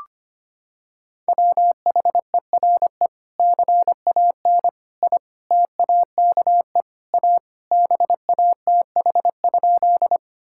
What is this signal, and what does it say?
Morse code, heard as a keyed tone, about 25 wpm, 700 Hz. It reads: WHERE CAN I TAKE A BATH?
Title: Sentences from Top 1600 Words 25wpm